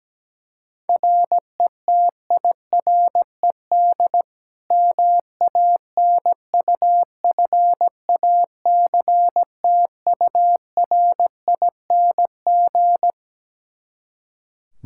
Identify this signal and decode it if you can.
Morse code, heard as a keyed tone, about 17 wpm, 700 Hz. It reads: RETIRED MANUFACTURING
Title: Practice QSO - Exchange 3 - More Info - 17wpm